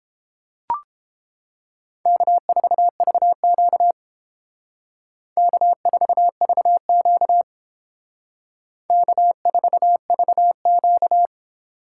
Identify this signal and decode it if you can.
Morse code, heard as a keyed tone, about 32 wpm, 700 Hz. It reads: K4VQ K4VQ K4VQ